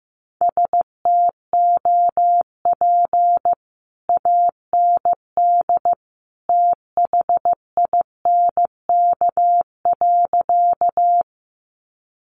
Morse code, heard as a keyed tone, about 15 wpm, 700 Hz. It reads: STOP AND THINK.
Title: Sentences from Top 300 Words 15wpm